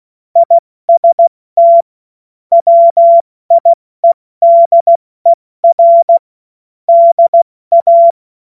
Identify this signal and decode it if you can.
Morse code, heard as a keyed tone, about 16 wpm, spaced as for 14 wpm, 680 Hz. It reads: IST WIEDER DA
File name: Morsekurs / Witze / Witz 16. November 2021, 70 F80